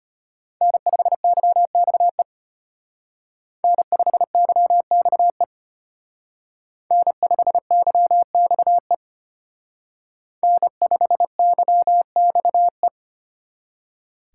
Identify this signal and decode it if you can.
Morse code, heard as a keyed tone, about 32 wpm, 700 Hz. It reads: N5YXE N5YXE N5YXE N5YXE